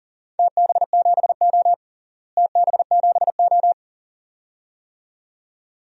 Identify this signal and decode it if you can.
Morse code, heard as a keyed tone, about 40 wpm, 700 Hz. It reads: TB7O TB7O
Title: DXCC-100 Call Signs - 4 Characters - Repeated 2x - 40wpm